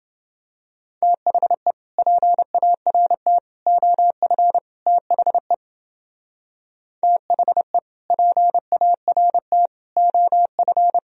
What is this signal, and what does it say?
Morse code, heard as a keyed tone, about 28 wpm, 700 Hz. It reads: THE PART OF THE THE PART OF